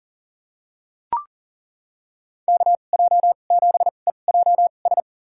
Morse code, heard as a keyed tone, about 37 wpm, spaced as for 28 wpm, 690 Hz. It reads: KJ7EJS
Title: USA Call Signs - 6 Characters - 2x Character Spacing - 40wpm